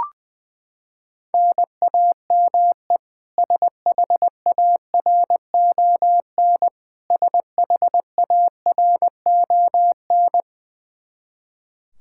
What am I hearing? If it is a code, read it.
Morse code, heard as a keyed tone, about 20 wpm, 700 Hz. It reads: NAME SHARON SHARON
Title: K1USN SST QSO Practice 20wpm